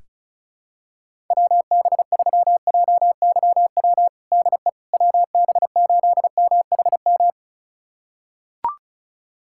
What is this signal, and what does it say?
Morse code, heard as a keyed tone, about 34 wpm, 700 Hz. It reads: WB3JYW DE WB8MHM
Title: USA Callsign DE Callsign - 6 Characters - 35wpm